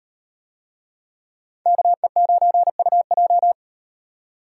Morse code, heard as a keyed tone, about 38 wpm, 700 Hz. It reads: KE9UJ